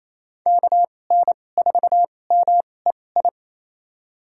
Morse code, heard as a keyed tone, about 28 wpm, spaced as for 20 wpm, 700 Hz. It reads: KN4MEI
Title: USA Call Signs - 6 Characters - 2x Character Spacing - 28wpm